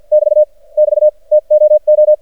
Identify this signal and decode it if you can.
Morse code, heard as a keyed tone, about 54 wpm, spaced as for 43 wpm, 610 Hz. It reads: <BT> <BT> TOO
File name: archived / CRSnet